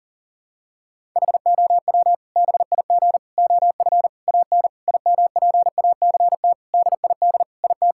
Morse code, heard as a keyed tone, about 40 wpm, 700 Hz. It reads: HOW BIG OF AN IMPACT DID IT
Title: Sentences from Top 1000 Words - Review Part 6 - 40wpm